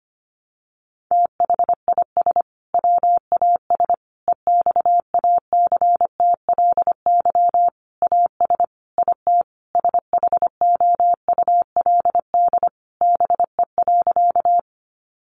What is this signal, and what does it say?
Morse code, heard as a keyed tone, about 25 wpm, 700 Hz. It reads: THIS WAS EXACTLY AS IT SHOULD BE.
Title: Sentences from Top 900 Words 25wpm